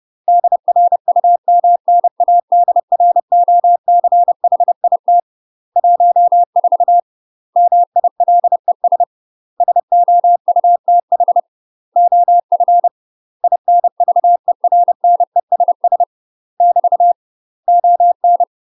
Morse code, heard as a keyed tone, about 30 wpm, 700 Hz. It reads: DRUMNADROCHIT 14 MILES SOUTH OF INVERNESS <BT> ON
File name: UK MASTER QSOs TRAINING / 30WPM / UK MASTER QSOs 0008